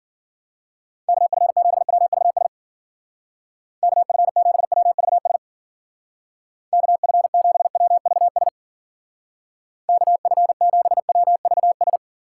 Morse code, heard as a keyed tone, about 49 wpm, 700 Hz. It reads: KF7WVS KF7WVS KF7WVS KF7WVS
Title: Speed Racing - USA 2x3 Call Signs 40wpm